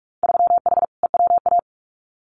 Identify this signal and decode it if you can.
Morse code, heard as a keyed tone, about 46 wpm, 700 Hz. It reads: 3H EWA
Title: Winter Field Day Exchanges Encore 45wpm